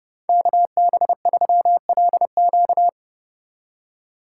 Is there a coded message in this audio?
Yes: Morse code from 0.3 to 2.9 s, heard as a keyed tone, about 34 wpm, 700 Hz. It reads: KB3LQ